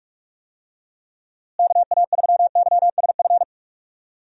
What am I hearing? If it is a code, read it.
Morse code, heard as a keyed tone, about 47 wpm, 700 Hz. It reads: KA3YSF
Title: USA Call Signs - 6 Characters - Repeated 3x - 45wpm